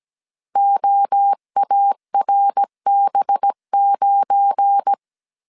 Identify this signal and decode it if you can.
Morse code, heard as a keyed tone, about 17 wpm, 800 Hz. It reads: OARB9
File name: CW / DET / 1 9 1500C S A 16wpm 2828